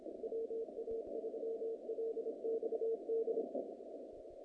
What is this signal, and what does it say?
Morse code, heard as a keyed tone, about 26 wpm, 450 Hz, with noise. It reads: WA3LXD
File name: contests / naqp-cw-k5tr-op-k5pi- -2200